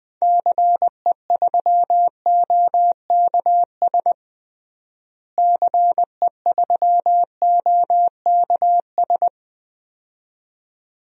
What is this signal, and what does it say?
Morse code, heard as a keyed tone, about 20 wpm, 700 Hz. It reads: CE3OKS CE3OKS
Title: DXCC-100 Call Signs - 6 Characters - Repeated 2x - 20wpm